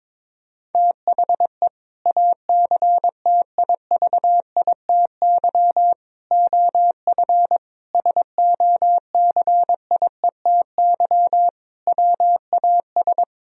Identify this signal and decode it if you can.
Morse code, heard as a keyed tone, about 22 wpm, 700 Hz. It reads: THE ACTIVITY OF SOCIETY WAS
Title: Sentences from Top 1000 Words - Review Part 4 - 22wpm